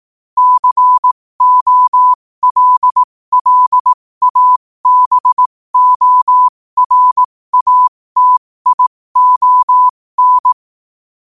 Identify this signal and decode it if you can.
Morse code, heard as a keyed tone, about 18 wpm, spaced as for 15 wpm, 1000 Hz. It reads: COLLABORATION
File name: qotdfiles / 15wpm